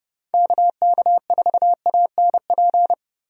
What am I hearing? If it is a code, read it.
Morse code, heard as a keyed tone, about 30 wpm, 700 Hz. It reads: KK4ANP
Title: USA Call Signs - 6 Characters - Repeated 2x - 30wpm